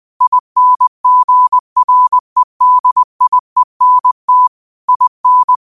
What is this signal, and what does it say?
Morse code, heard as a keyed tone, about 20 wpm, 1000 Hz. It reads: INGREDIENT IN
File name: qotdfiles / 20wpm